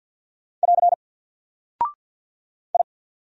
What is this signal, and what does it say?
Morse code, heard as a keyed tone, about 48 wpm, 700 Hz. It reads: <AR> I
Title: Single Character (Letter:Number:Punctuation:Prosign) Alt 50wpm